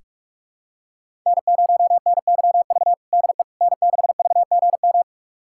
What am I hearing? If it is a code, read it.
Morse code, heard as a keyed tone, about 46 wpm, 700 Hz. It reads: N0NYV DE N6VGM
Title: USA Callsign DE Callsign - 5 Characters - 45wpm